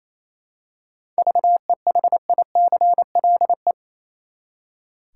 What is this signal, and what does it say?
Morse code, heard as a keyed tone, about 28 wpm, 700 Hz. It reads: VEHICLE
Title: Seven Letter Words - 28wpm